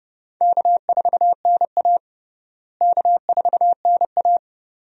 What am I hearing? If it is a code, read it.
Morse code, heard as a keyed tone, about 29 wpm, 700 Hz. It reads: K4NA K4NA
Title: USA Call Signs - 4 Characters - Repeated 2x - 30wpm